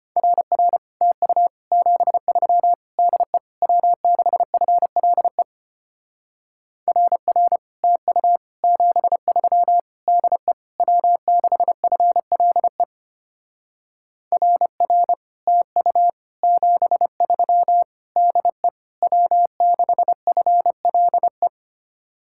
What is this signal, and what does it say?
Morse code, heard as a keyed tone, about 33 wpm, 700 Hz. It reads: RR TU 73 DE W6FLE RR TU 73 DE W6FLE RR TU 73 DE W6FLE